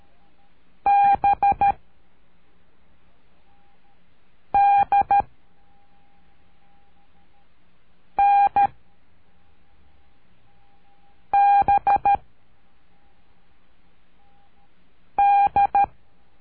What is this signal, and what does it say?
Morse code, heard as a keyed tone, about 13 wpm, 800 Hz. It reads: B D N B D
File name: cursos / CURSO DE TELEGRAFIA / Telegrafia - Trilha 03